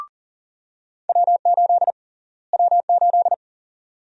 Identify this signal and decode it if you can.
Morse code, heard as a keyed tone, about 40 wpm, 700 Hz. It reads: W8 W8
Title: USA Call Sign Prefixes - Repeated 2x - 40wpm